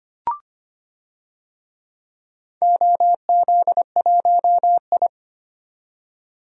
Morse code, heard as a keyed tone, about 25 wpm, 700 Hz. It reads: OZ1I
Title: DXCC-100 Call Signs - 4 Characters - 25wpm